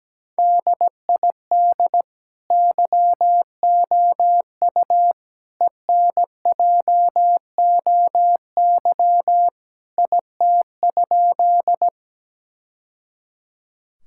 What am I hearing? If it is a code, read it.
Morse code, heard as a keyed tone, about 17 wpm, 700 Hz. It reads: DID YOU ENJOY IT?
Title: Sentences from Top 1000 Words - Review Part 5 - 17wpm